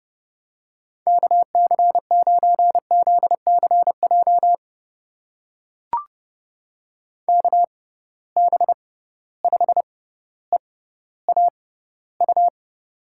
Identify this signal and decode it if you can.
Morse code, heard as a keyed tone, about 32 wpm, spaced as for 16 wpm, 700 Hz. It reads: KC9ZCJ KB5EAU
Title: USA Call Signs - 6 Characters - 6x Character Spacing - 30wpm